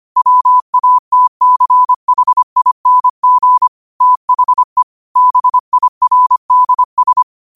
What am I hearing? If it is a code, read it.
Morse code, heard as a keyed tone, about 25 wpm, 1000 Hz. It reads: WATCHING THE BIRDS
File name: qotdfiles / 25wpm